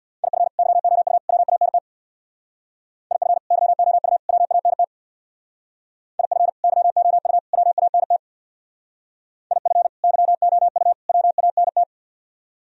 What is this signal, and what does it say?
Morse code, heard as a keyed tone, about 62 wpm, 700 Hz. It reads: IF YOU WANT IF YOU WANT IF YOU WANT IF YOU WANT
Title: Speed-Racing - Sets of 3 Words 50wpm